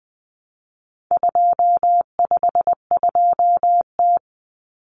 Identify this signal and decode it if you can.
Morse code, heard as a keyed tone, about 20 wpm, 700 Hz. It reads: 252T
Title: Common Cut Numbers - 4-Digit 20wpm